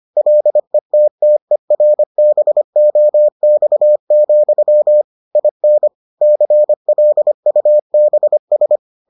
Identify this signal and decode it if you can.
Morse code, heard as a keyed tone, about 25 wpm, 600 Hz. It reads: LETTERBOX, IN CLUBS